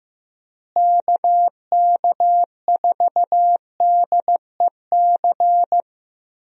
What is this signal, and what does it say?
Morse code, heard as a keyed tone, about 15 wpm, 700 Hz. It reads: KK4DEC
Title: USA Call Signs - 6 Characters - Repeated 2x - 15wpm